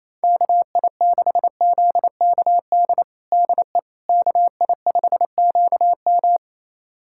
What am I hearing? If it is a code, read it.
Morse code, heard as a keyed tone, about 28 wpm, 700 Hz. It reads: KI6ZKD DE KI5QM
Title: USA Callsign DE Callsign 28wpm